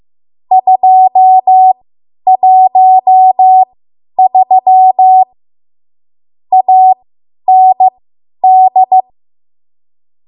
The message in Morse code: 213 AND

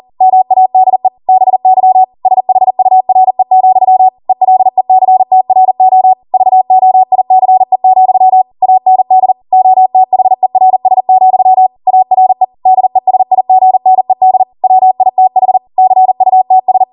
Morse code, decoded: MADE BY SHURE, ELECTRO VOICE, AND OTHERS, ARE DESIGNED WITH CUTS

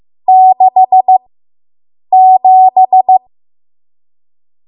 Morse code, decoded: 67

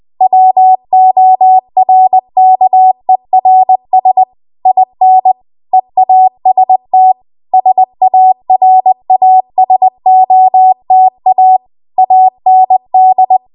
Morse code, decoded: WORKERS IN EAST SARASOTA AND

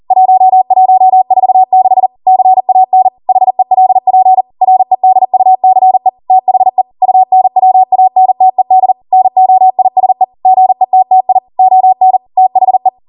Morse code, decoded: 1146 CAN HELP REDUCE THE UNWANTED NOISE GETTI ON THE

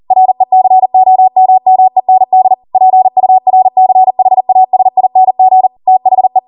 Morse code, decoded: RECOMMEND PURCHASING THE

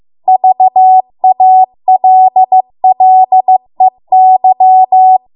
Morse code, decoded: VALLEY

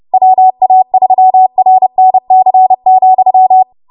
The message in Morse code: WA3RNC,

